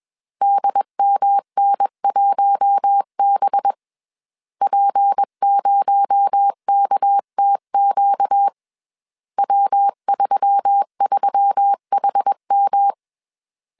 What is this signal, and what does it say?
Morse code, heard as a keyed tone, about 21 wpm, 800 Hz. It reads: DMN1B P0KTQ W33HM
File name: CW / DET / 1 6 1500C S A 20wpm 2048